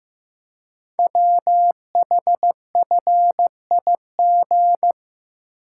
Morse code, heard as a keyed tone, about 15 wpm, 700 Hz. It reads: WHFIG
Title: Code Groups - 5 Letters - 15wpm